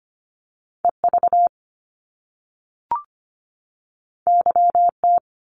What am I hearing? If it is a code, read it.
Morse code, heard as a keyed tone, about 25 wpm, 700 Hz. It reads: EV YT
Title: Top Two Letter Combinations in Words 25wpm